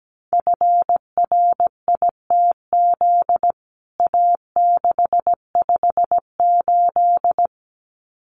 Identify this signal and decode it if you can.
Morse code, heard as a keyed tone, about 17 wpm, 700 Hz. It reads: FRITZ A658
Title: CWT Exchange 17wpm